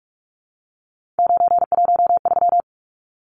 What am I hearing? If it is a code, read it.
Morse code, heard as a keyed tone, about 44 wpm, 700 Hz. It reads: 913
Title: Numbers - 3-Digit 45wpm